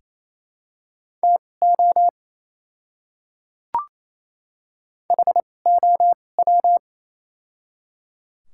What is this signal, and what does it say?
Morse code, heard as a keyed tone, about 28 wpm, spaced as for 20 wpm, 700 Hz. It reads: TO HOW
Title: 2x Character Spacing - Top 100 Words - 28wpm